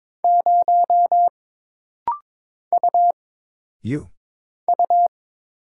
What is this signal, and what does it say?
Morse code, heard as a keyed tone, about 22 wpm, 700 Hz. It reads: 0 U U
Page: This rapid-fire timing is geared to push your ICR (Instant Character Recognition) skills.